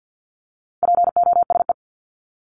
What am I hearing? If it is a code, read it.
Morse code, heard as a keyed tone, about 50 wpm, 700 Hz. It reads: POSE
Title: Top 500 Words Repeated 50wpm